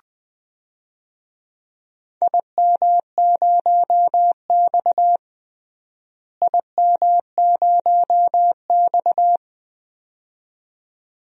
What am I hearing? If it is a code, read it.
Morse code, heard as a keyed tone, about 20 wpm, 700 Hz. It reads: IM0X IM0X
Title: DXCC-100 Call Signs - 4 Characters - Repeated 2x - 20wpm